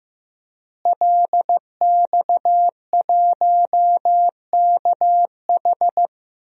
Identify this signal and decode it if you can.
Morse code, heard as a keyed tone, about 15 wpm, 700 Hz. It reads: LX1KH
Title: DXCC-100 Call Signs - 5 Characters - Repeated 3x - 15wpm